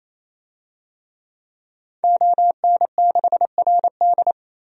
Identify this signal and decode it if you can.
Morse code, heard as a keyed tone, about 28 wpm, 700 Hz. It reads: ON6RD
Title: DXCC-100 Call Signs - 5 Characters - 28wpm